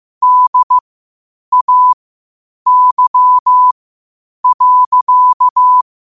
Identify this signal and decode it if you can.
Morse code, heard as a keyed tone, about 15 wpm, spaced as for 8 wpm, 1000 Hz. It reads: DAY.
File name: qotdfiles / 8wpm